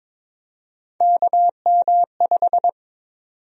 Morse code, heard as a keyed tone, about 22 wpm, 700 Hz. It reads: KM5
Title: USA Call Sign Prefixes - 22wpm